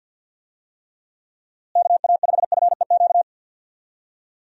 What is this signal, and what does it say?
Morse code, heard as a keyed tone, about 51 wpm, 700 Hz. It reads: KA5FEQ
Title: USA Call Signs - 6 Characters - Repeated 3x - 50wpm